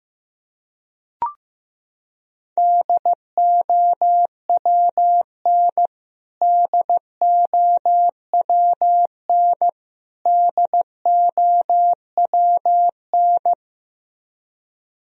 Morse code, heard as a keyed tone, about 15 wpm, 700 Hz. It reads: DOWN DOWN DOWN
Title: Top 100 Words Repeated 15wpm